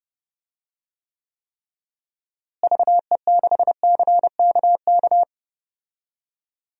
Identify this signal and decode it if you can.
Morse code, heard as a keyed tone, about 30 wpm, 700 Hz. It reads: VE6CKK
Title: DXCC-100 Call Signs - 6 Characters - Repeated 3x - 30wpm